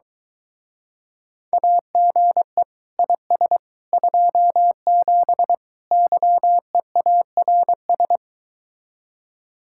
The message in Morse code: AGE IS 27 YEARS

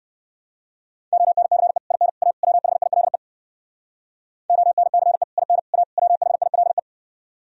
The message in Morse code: ONCE IN A WHILE ONCE IN A WHILE

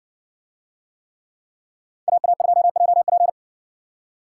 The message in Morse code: AA2JP

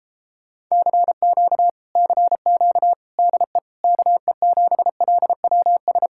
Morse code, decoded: CQ CQ DE KE7LWS

